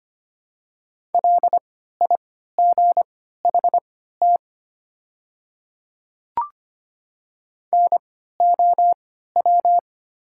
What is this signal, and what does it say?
Morse code, heard as a keyed tone, about 25 wpm, spaced as for 14 wpm, 700 Hz. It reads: LIGHT NOW
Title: 3x Character Spacing - Top 200 Words - 25wpm